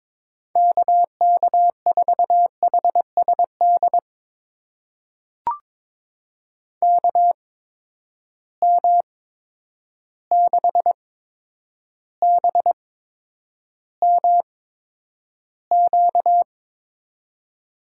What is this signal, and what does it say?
Morse code, heard as a keyed tone, about 22 wpm, spaced as for 9 wpm, 700 Hz. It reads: KK4HSD KM6BMQ
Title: USA Call Signs - 8x Character Spacing - 22wpm